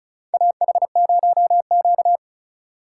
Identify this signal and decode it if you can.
Morse code, heard as a keyed tone, about 36 wpm, 700 Hz. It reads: AH0Q
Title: USA Call Signs - 4 Characters - 2x Character Spacing - 35wpm